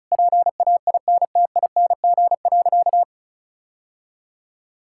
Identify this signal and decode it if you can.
Morse code, heard as a keyed tone, about 34 wpm, 690 Hz. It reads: PAINTING.
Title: Sentences from Top 1000 Words - Review Part 6 - 35wpm